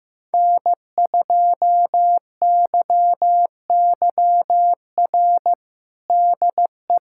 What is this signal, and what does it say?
Morse code, heard as a keyed tone, about 15 wpm, 700 Hz. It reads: N2YYR DE
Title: USA Callsign DE Callsign - 5 Characters - 15wpm